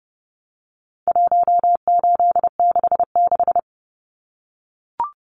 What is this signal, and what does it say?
Morse code, heard as a keyed tone, about 30 wpm, 700 Hz. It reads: 1866
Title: Numbers - 4-Digit 30wpm